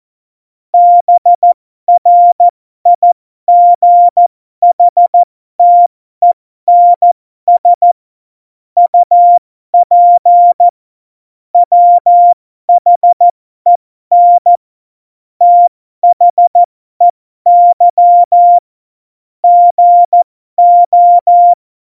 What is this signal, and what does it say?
Morse code, heard as a keyed tone, about 14 wpm, spaced as for 12 wpm, 700 Hz. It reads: BRIGHTENS UP WHEN THEY GO
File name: Morsekurs / Witze